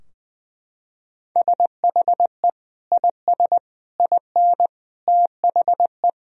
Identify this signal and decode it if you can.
Morse code, heard as a keyed tone, about 21 wpm, 700 Hz. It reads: SHE IS IN THE
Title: Sentences from Top 1000 Words - Review Part 4 - 20wpm